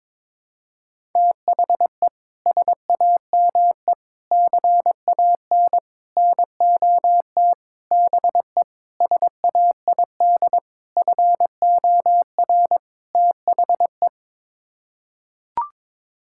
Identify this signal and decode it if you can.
Morse code, heard as a keyed tone, about 22 wpm, 700 Hz. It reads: THE SAME CAN NOT BE SAID FOR THE
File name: practice / 22wpm / Sets of 8 Words - 22wpm